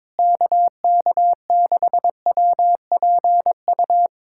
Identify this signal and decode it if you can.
Morse code, heard as a keyed tone, about 22 wpm, 700 Hz. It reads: KK6WPU
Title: USA Callsign DE Callsign 22wpm